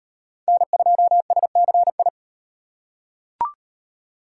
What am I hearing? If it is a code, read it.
Morse code, heard as a keyed tone, about 34 wpm, 700 Hz. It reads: N2SCI